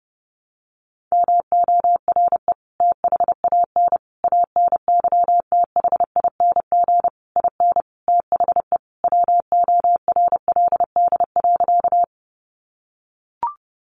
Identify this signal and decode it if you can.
Morse code, heard as a keyed tone, about 30 wpm, 700 Hz. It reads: MORE THAN ANYTHING IN THE WORLD.
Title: Sentences from Top 1000 Words 30wpm